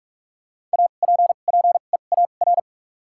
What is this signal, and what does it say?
Morse code, heard as a keyed tone, about 43 wpm, spaced as for 32 wpm, 700 Hz. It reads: APPEAR
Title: 2x Character Spacing - Top 500 Words - 45wpm